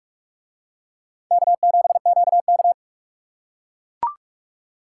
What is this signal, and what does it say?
Morse code, heard as a keyed tone, about 44 wpm, 700 Hz. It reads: K7QK